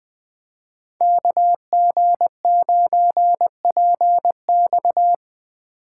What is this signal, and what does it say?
Morse code, heard as a keyed tone, about 24 wpm, 700 Hz. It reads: KG9PX
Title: Speed Racing - USA 2x2 Call Signs 20wpm